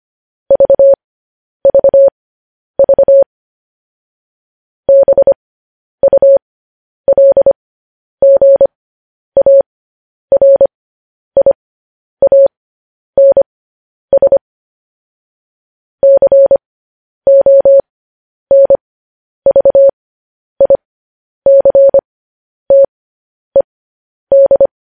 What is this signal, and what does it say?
Morse code, heard as a keyed tone, about 25 wpm, spaced as for 10 wpm, 550 Hz. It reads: VVV BULGARIANS CONVICTED
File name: podcast / News headlines-1x-25x10